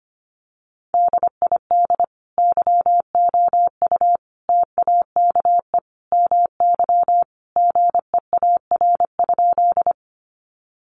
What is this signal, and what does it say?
Morse code, heard as a keyed tone, about 25 wpm, 700 Hz. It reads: DID YOU TAKE MY GEAR?
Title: Sentences from Top 1400 Words 25wpm